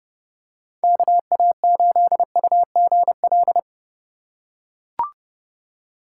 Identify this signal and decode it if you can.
Morse code, heard as a keyed tone, about 31 wpm, 700 Hz. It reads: KA8UGL